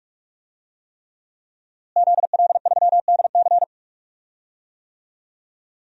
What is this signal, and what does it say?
Morse code, heard as a keyed tone, about 44 wpm, 700 Hz. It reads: ZL3DC
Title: DXCC-100 Call Signs - 5 Characters - Repeated 3x - 45wpm